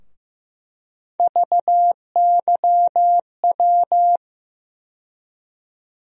Morse code, heard as a keyed tone, about 15 wpm, 700 Hz. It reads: VYW
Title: USA Call Sign Postfixes - 3 Letters 15wpm